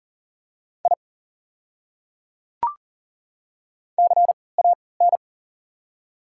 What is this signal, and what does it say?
Morse code, heard as a keyed tone, about 40 wpm, spaced as for 23 wpm, 690 Hz. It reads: I CAN
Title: 3x Character Spacing - Top 200 Words - 40wpm